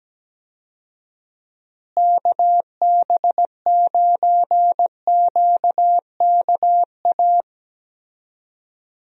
Morse code, heard as a keyed tone, about 17 wpm, 700 Hz. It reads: KB9QKA
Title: USA Call Signs - 6 Characters - Repeated 2x - 17wpm